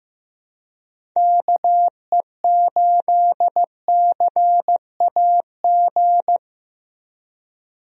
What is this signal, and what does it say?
Morse code, heard as a keyed tone, about 15 wpm, 700 Hz. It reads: KE8CAG
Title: USA Call Signs - 6 Characters - Repeated 2x - 15wpm